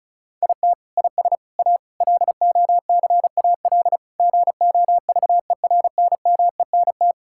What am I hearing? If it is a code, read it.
Morse code, heard as a keyed tone, about 34 wpm, 700 Hz. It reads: IT IS A LOCAL GOVERNMENT
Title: Sentences from Top 1000 Words - Review Part 5 - 35wpm